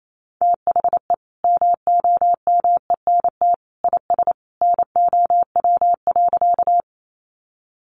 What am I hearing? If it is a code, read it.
Morse code, heard as a keyed tone, about 28 wpm, 700 Hz. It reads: THE MOMENT IS NOW.
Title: Sentences from Top 1000 Words 28wpm